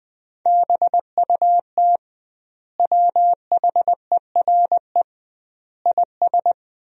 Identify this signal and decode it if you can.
Morse code, heard as a keyed tone, about 20 wpm, 700 Hz. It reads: BUT WHERE IS
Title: 2x Word-Spacing - Sentences from Top 200 Words - 20wpm